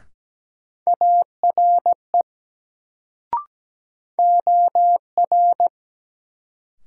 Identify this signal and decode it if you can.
Morse code, heard as a keyed tone, about 17 wpm, 700 Hz. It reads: ARE OR
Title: Sets of 2 Words - New Word Drill - 17wpm